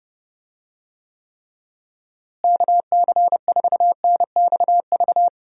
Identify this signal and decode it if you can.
Morse code, heard as a keyed tone, about 30 wpm, 700 Hz. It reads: KC4NXV
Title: USA Call Signs - 6 Characters - Repeated 3x - 30wpm